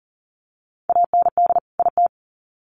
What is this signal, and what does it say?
Morse code, heard as a keyed tone, about 40 wpm, 700 Hz. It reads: AND IT
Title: Sets of 2 Words 40wpm